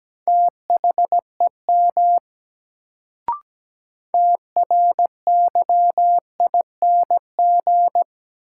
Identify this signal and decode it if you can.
Morse code, heard as a keyed tone, about 17 wpm, 700 Hz. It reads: THEM TRYING
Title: Sets of 2 Words - New Word Drill - 17wpm